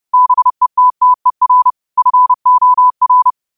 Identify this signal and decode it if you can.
Morse code, heard as a keyed tone, about 30 wpm, 1000 Hz. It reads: BETTER FOR
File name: qotdfiles / 30wpm